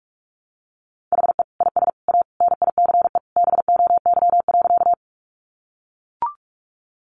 Morse code, heard as a keyed tone, about 44 wpm, 700 Hz. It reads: HE IS A NICE BOY.
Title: Sentences from Top 1100 Words 45wpm